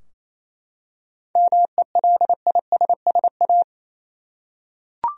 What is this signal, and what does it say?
Morse code, heard as a keyed tone, about 28 wpm, 700 Hz. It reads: MELISSA
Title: USA Common Names - 28wpm